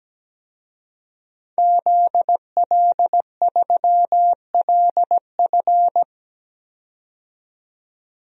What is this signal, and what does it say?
Morse code, heard as a keyed tone, about 17 wpm, 700 Hz. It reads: ZL3LF
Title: DXCC-100 Call Signs - 5 Characters - 17wpm